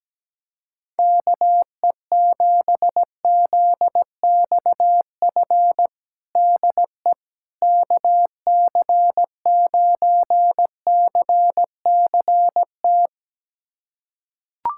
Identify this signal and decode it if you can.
Morse code, heard as a keyed tone, about 17 wpm, 700 Hz. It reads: KE7ZXF DE KC9CCT
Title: USA Callsign DE Callsign - 6 Characters - 17wpm